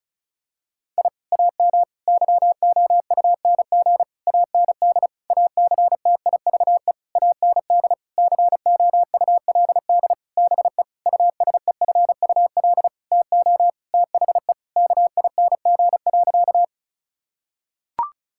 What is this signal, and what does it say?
Morse code, heard as a keyed tone, about 36 wpm, 700 Hz. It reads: I AM YOUNG AND ACTIVE AND COULD BE USEFUL TO THE KING.
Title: Sentences from Top 1000 Words - Review Part 6 - 35wpm